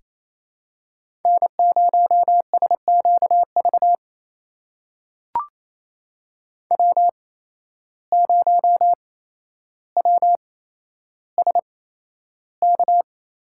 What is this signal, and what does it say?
Morse code, heard as a keyed tone, about 27 wpm, spaced as for 12 wpm, 700 Hz. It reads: N0SQV W0WSK
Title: USA Call Signs - 5 Characters - 8x Character Spacing - 28wpm